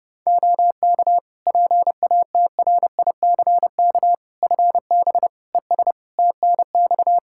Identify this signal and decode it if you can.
Morse code, heard as a keyed tone, about 30 wpm, 700 Hz. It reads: OK PATRICK FB ES TNX
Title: Practice QSO - Exchange 2 - Info - 30wpm